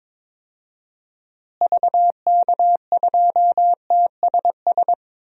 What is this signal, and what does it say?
Morse code, heard as a keyed tone, about 22 wpm, 700 Hz. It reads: VK2TSS
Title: DXCC-100 Call Signs - 6 Characters - Repeated 2x - 22wpm